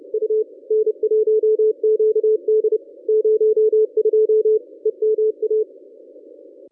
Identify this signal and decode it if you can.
Morse code, heard as a keyed tone, about 30 wpm, 430 Hz. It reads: U N1QD 02 EMA
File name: contests / ss-cw-k5tr-op-n5rz -1000